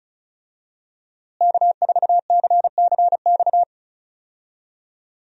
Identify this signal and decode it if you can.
Morse code, heard as a keyed tone, about 34 wpm, 700 Hz. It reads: K4CCX
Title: USA Call Signs - 5 Characters - Repeated 3x - 35wpm